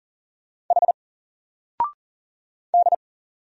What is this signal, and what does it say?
Morse code, heard as a keyed tone, about 40 wpm, 700 Hz. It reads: H D
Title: Single Letter-Number Alt 40wpm